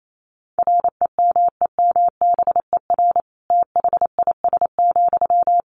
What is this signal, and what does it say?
Morse code, heard as a keyed tone, about 28 wpm, 700 Hz. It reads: REMEMBER THIS,
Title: Sentences from Top 1500 Words 28wpm